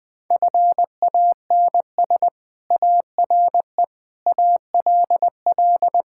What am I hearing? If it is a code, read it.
Morse code, heard as a keyed tone, about 20 wpm, 700 Hz. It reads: FANS ARE ALL
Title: Sentences with 4-Word Phrases - 20wpm